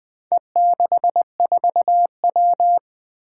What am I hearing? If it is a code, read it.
Morse code, heard as a keyed tone, about 20 wpm, 700 Hz. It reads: E64W
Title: Four Challenging Letter-Number Combinations 20wpm